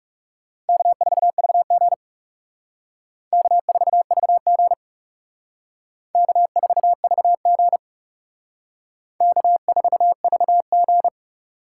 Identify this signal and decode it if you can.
Morse code, heard as a keyed tone, about 36 wpm, 700 Hz. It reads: K4VG K4VG K4VG K4VG
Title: Speed Racing - USA Call Signs 30wpm